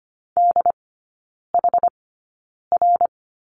Morse code, heard as a keyed tone, about 25 wpm, spaced as for 9 wpm, 700 Hz. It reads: DHR
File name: Lesson 17e - Letter u - Three Characters 25wpm